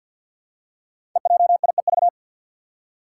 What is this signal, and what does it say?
Morse code, heard as a keyed tone, about 53 wpm, 700 Hz. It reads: EJIEV
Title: Five Challenging Letter-Number Combinations 50wpm